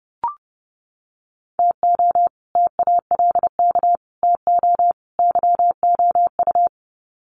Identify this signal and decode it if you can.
Morse code, heard as a keyed tone, about 30 wpm, 700 Hz. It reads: TO TALK TO YOU
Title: Sets of 4 Words 30wpm